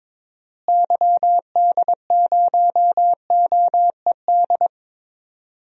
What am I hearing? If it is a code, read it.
Morse code, heard as a keyed tone, about 22 wpm, 700 Hz. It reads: YD0OED
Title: DXCC-100 Call Signs - 6 Characters - Repeated 3x - 22wpm